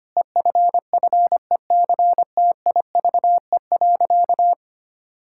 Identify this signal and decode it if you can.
Morse code, heard as a keyed tone, about 25 wpm, 700 Hz. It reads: EFFECTIVE.
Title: Sentences from Top 1000 Words - Review Part 6 - 25wpm